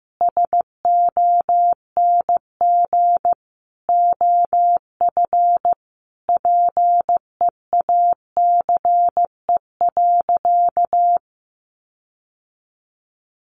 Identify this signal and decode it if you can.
Morse code, heard as a keyed tone, about 15 wpm, 700 Hz. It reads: SONG OF PEACE.
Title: Sentences from Top 1500 Words 15wpm